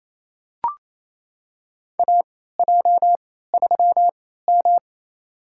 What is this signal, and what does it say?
Morse code, heard as a keyed tone, about 28 wpm, spaced as for 16 wpm, 700 Hz. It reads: AJ3M
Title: USA Call Signs - 3x Character Spacing - 28wpm